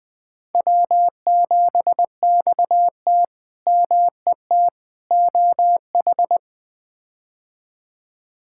Morse code, heard as a keyed tone, about 20 wpm, 700 Hz. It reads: W7XT MET OH
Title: K1USN SST QSO Practice - 20wpm